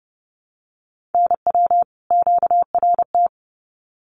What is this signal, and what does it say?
Morse code, heard as a keyed tone, about 30 wpm, 700 Hz. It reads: NW QRT
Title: Common QSO Phrases 30wpm